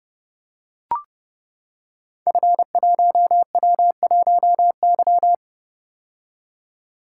Morse code, heard as a keyed tone, about 30 wpm, 700 Hz. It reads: F1W1Y
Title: Code Groups - 5 Letter:Numbers - 30wpm